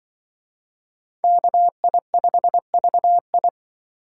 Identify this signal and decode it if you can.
Morse code, heard as a keyed tone, about 25 wpm, 700 Hz. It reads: KI5VI
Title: Speed Racing - USA 2x2 Call Signs 20wpm